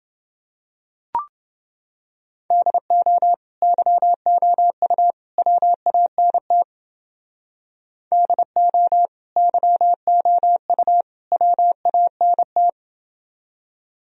Morse code, heard as a keyed tone, about 28 wpm, 700 Hz. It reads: DO YOU WANT DO YOU WANT